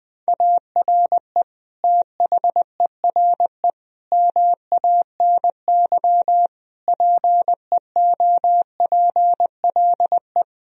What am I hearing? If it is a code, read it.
Morse code, heard as a keyed tone, about 20 wpm, 700 Hz. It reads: ARE THERE MANY PEOPLE
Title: Sentences from Top 1000 Words - Review Part 1 - 20wpm